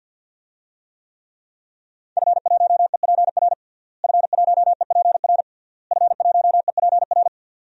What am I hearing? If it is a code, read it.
Morse code, heard as a keyed tone, about 50 wpm, 700 Hz. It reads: F1EPR F1EPR F1EPR
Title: DXCC-100 Call Signs - 5 Characters - Repeated 3x - 50wpm